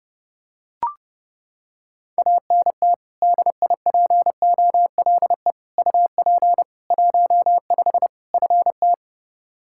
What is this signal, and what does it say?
Morse code, heard as a keyed tone, about 30 wpm, 700 Hz. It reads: ANT DIPOLE UP 15 FT
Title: Practice QSO - Exchange 2 - Info - 30wpm